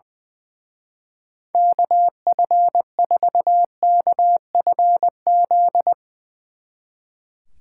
Morse code, heard as a keyed tone, about 22 wpm, 700 Hz. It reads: KF4KFZ
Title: Speed Racing - USA 2x3 Call Signs 20wpm